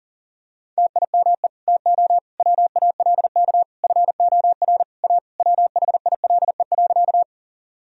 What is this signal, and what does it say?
Morse code, heard as a keyed tone, about 40 wpm, 700 Hz. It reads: TIME TO WALK FOR A WHILE.
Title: Sentences from Top 1000 Words - Review Part 1 - 40wpm